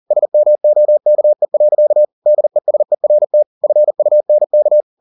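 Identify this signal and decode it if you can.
Morse code, heard as a keyed tone, about 40 wpm, 600 Hz. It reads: SMOKE. DESERT FUNK
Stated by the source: Morse code at 40 wpm, practice text, CW med Gnister.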